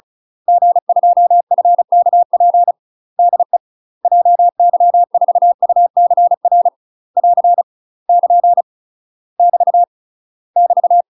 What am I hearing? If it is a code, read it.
Morse code, heard as a keyed tone, about 35 wpm, 700 Hz. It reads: G2FKP DE JY4UCR <AR> <KN> <BT> <BT>
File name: UK MASTER QSOs TRAINING / 35WPM / UK MASTER QSOs 0022